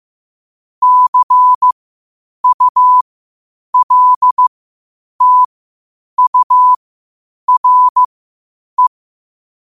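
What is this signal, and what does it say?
Morse code, heard as a keyed tone, about 15 wpm, spaced as for 8 wpm, 1000 Hz. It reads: CULTURE
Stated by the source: Morse code at 8 wpm.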